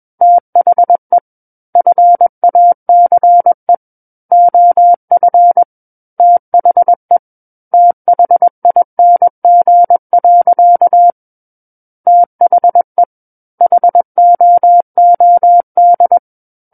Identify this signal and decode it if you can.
Morse code, heard as a keyed tone, about 21 wpm, 700 Hz. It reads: THE FACE OF THE THING. THE HOOD
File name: Book One / 12-Chapter 12 - 21 WPM